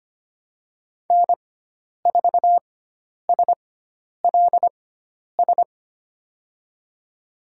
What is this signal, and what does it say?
Morse code, heard as a keyed tone, about 25 wpm, spaced as for 10 wpm, 700 Hz. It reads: N4SLS
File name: Lesson 16g - Letter c - Call Signs 25wpm